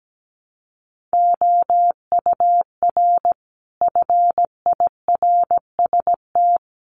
Morse code, heard as a keyed tone, about 17 wpm, 700 Hz. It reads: OUR FIRST
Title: Sentences from Top 1300 Words 17wpm